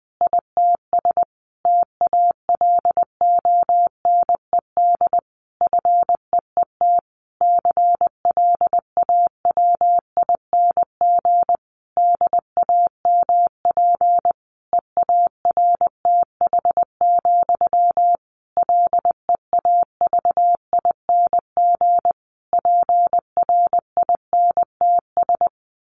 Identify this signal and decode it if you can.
Morse code, heard as a keyed tone, about 20 wpm, 700 Hz. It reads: ITS TALONED FEET CLAWING DAMP EARTH, LEAVING PRINTS
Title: A Sound Of Thunder 20wpm